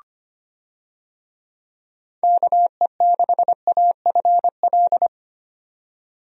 Morse code, heard as a keyed tone, about 25 wpm, 700 Hz. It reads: KE6AFL
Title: USA Call Signs - 6 Characters - 25wpm